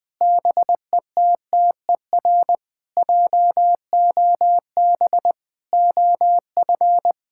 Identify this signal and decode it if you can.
Morse code, heard as a keyed tone, about 20 wpm, 700 Hz. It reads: BETTER JOB OF
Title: Sentences with 6-Word Phrases - 20wpm